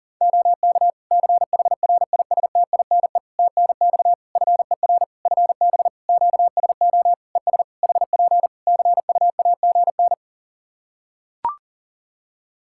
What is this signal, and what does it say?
Morse code, heard as a keyed tone, about 42 wpm, 700 Hz. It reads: OK CHRISTINE TNX FER FB QSO ES HP CUAGN
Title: Practice QSO - Exchange 4 - Ending - 40wpm